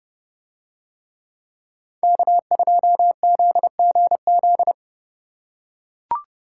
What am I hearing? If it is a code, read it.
Morse code, heard as a keyed tone, about 29 wpm, 700 Hz. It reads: K2ZGZ